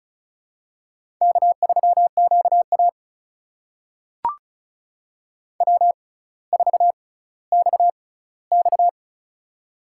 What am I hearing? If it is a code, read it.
Morse code, heard as a keyed tone, about 36 wpm, spaced as for 18 wpm, 700 Hz. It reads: K3QA W4XX
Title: USA Call Signs - 6x Character Spacing - 35wpm